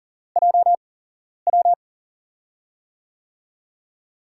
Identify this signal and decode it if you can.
Morse code, heard as a keyed tone, about 40 wpm, spaced as for 11 wpm, 700 Hz. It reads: JW
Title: USA Call Signs - 4 Characters - 8x Character Spacing - 40wpm